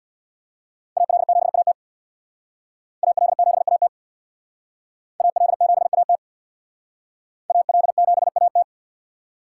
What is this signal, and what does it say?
Morse code, heard as a keyed tone, about 60 wpm, 700 Hz. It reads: AL7AT AL7AT AL7AT AL7AT
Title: Speed Racing - USA 2x2 Call Signs 50wpm